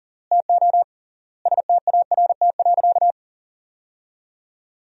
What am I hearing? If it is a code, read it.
Morse code, heard as a keyed tone, about 40 wpm, 700 Hz. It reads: TO START.
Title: 3x Word-Spacing - Sentences from Top 300 Words - 40wpm